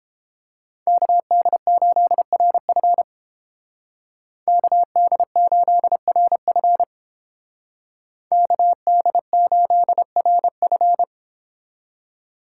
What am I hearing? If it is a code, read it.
Morse code, heard as a keyed tone, about 29 wpm, 700 Hz. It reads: KD8RF KD8RF KD8RF